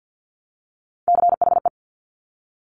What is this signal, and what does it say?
Morse code, heard as a keyed tone, about 50 wpm, 700 Hz. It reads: CHE
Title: Top Three Letter Combinations in Words 50wpm